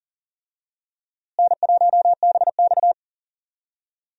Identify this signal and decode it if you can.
Morse code, heard as a keyed tone, about 40 wpm, 700 Hz. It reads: N1BX
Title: USA Call Signs - 4 Characters - Repeated 3x - 40wpm